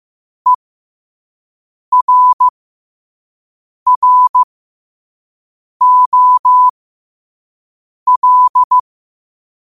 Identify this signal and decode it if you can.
Morse code, heard as a keyed tone, about 15 wpm, spaced as for 5 wpm, 1000 Hz. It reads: ERROL
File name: qotdfiles / 5wpm